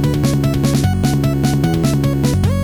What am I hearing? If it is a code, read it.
There is no code.